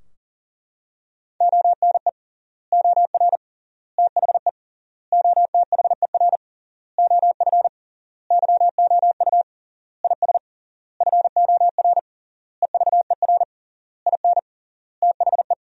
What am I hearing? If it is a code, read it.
Morse code, heard as a keyed tone, about 37 wpm, 700 Hz. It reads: ONE OR THE OTHER OF YOU IS FOR EVER IN THE
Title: 3x Word-Spacing - Sentences from Top 400 Words - 40wpm